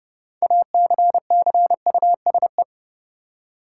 Morse code, heard as a keyed tone, about 31 wpm, 700 Hz. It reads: ACCUSE
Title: Six Letter Words - 30wpm